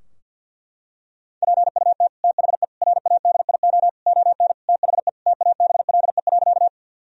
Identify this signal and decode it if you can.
Morse code, heard as a keyed tone, about 51 wpm, 700 Hz. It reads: PUT THE RADIO ON THE TABLE.
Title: Sentences from Top 1000 Words - Review Part 3 - 50wpm